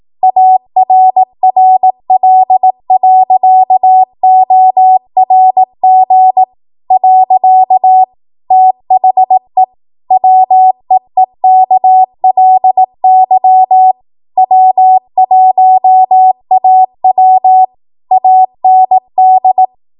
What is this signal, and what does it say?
Morse code, heard as a keyed tone, about 18 wpm, 750 Hz. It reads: ARRL.ORG . THE WEEKLY W1AW AND
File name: Archive / 18 WPM / 18WPM